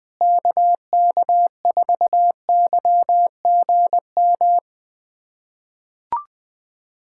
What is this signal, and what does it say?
Morse code, heard as a keyed tone, about 20 wpm, 700 Hz. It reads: KK4YGM
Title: USA Callsign DE Callsign - 6 Characters - 20wpm